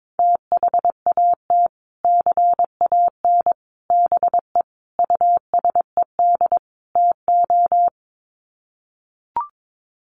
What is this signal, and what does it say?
Morse code, heard as a keyed tone, about 22 wpm, 700 Hz. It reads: THAT CAN BE USED TO
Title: Sets of 5 Words Encore 22wpm